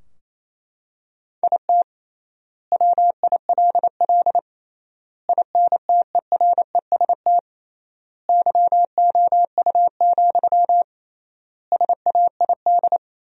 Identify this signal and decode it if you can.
Morse code, heard as a keyed tone, about 28 wpm, 700 Hz. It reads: IT WILL INTEREST YOU, SAID
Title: 3x Word-Spacing - Sentences from Top 400 Words - 28wpm